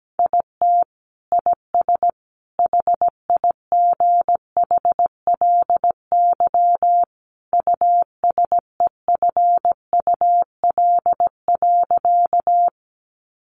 Morse code, heard as a keyed tone, about 17 wpm, 700 Hz. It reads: IT IS HIGHLY USEFUL.
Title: Sentences from Top 1100 Words 17wpm